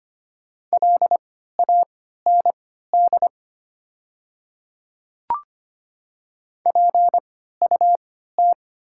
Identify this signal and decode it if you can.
Morse code, heard as a keyed tone, about 25 wpm, spaced as for 14 wpm, 690 Hz. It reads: LAND PUT
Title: 3x Character Spacing - Top 500 Words - 25wpm